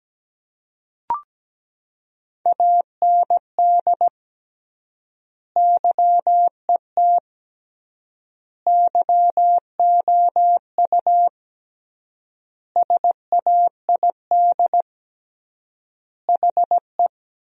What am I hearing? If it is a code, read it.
Morse code, heard as a keyed tone, about 17 wpm, 700 Hz. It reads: AND YET YOU SAID HE
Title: 3x Word-Spacing - Sentences from Top 500 Words - 17wpm